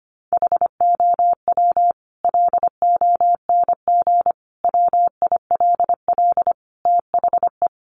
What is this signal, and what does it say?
Morse code, heard as a keyed tone, about 25 wpm, 700 Hz. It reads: HOW LONG WILL THE
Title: Sentences from Top 1200 Words 25wpm